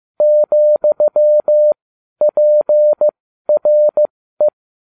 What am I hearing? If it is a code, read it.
Morse code, heard as a keyed tone, about 15 wpm, spaced as for 12 wpm, 600 Hz. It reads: ,PRE